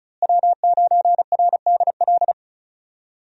Morse code, heard as a keyed tone, about 34 wpm, 700 Hz. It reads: W9RDL
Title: USA Call Signs - 5 Characters - Repeated 2x - 35wpm